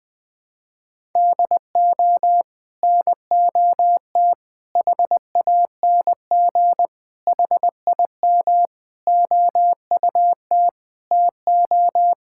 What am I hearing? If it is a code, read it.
Morse code, heard as a keyed tone, about 20 wpm, 700 Hz. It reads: DO NOT HANG HIM OUT TO
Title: Sentences from Top 1600 Words 20wpm